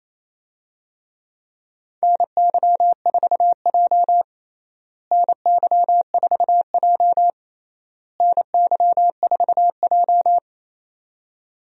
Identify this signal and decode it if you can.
Morse code, heard as a keyed tone, about 28 wpm, 700 Hz. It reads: NY4J NY4J NY4J
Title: USA Call Signs - 4 Characters - Repeated 3x - 28wpm